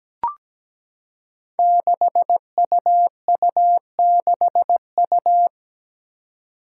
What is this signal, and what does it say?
Morse code, heard as a keyed tone, about 17 wpm, 700 Hz. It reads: 6UU6U
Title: Five Challenging Letter-Number Combinations 17wpm